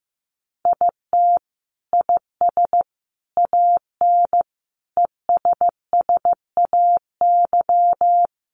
Morse code, heard as a keyed tone, about 15 wpm, 700 Hz. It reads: IT IS AN ESSAY
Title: Sentences from Top 1500 Words 15wpm